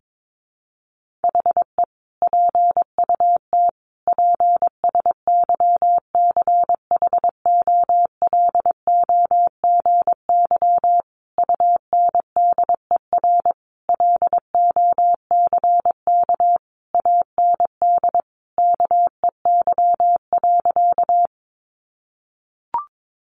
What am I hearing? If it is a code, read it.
Morse code, heard as a keyed tone, about 22 wpm, 700 Hz. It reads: HE PUT PSYCHOLOGY UNDER LOCK AND KEY.
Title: Sentences from Top 1400 Words 22wpm